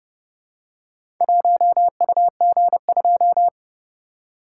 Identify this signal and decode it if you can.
Morse code, heard as a keyed tone, about 29 wpm, 700 Hz. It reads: 1UG2